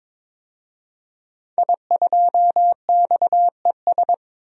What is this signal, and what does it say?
Morse code, heard as a keyed tone, about 22 wpm, 700 Hz. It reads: I2XES
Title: DXCC-100 Call Signs - 5 Characters - 22wpm